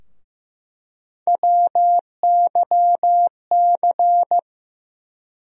Morse code, heard as a keyed tone, about 15 wpm, 700 Hz. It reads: WYC
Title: USA Call Sign Postfixes - 3 Letters 15wpm